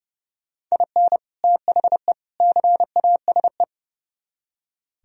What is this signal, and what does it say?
Morse code, heard as a keyed tone, about 36 wpm, 700 Hz. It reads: IN THE CASE
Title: Speed-Racing - Sets of 3 Words 30wpm